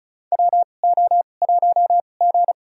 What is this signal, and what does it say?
Morse code, heard as a keyed tone, about 36 wpm, spaced as for 25 wpm, 700 Hz. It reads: WO1G
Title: USA Call Signs - 4 Characters - 2x Character Spacing - 35wpm